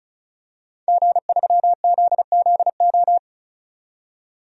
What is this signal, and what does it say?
Morse code, heard as a keyed tone, about 34 wpm, 700 Hz. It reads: G3ZZO
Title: DXCC-100 Call Signs - 5 Characters - 35wpm